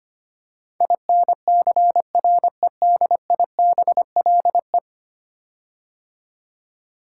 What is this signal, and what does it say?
Morse code, heard as a keyed tone, about 25 wpm, 700 Hz. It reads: INCREDIBLE
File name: 10 Letter Words - First 3 Letters Given - 25wpm